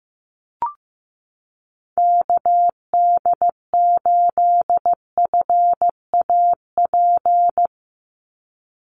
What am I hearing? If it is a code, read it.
Morse code, heard as a keyed tone, about 15 wpm, 700 Hz. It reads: KD8FAP
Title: USA Call Signs 15wpm